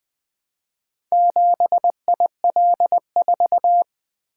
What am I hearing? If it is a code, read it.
Morse code, heard as a keyed tone, about 20 wpm, 700 Hz. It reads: 7IL4
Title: Four Letter-Number Combinations - 20wpm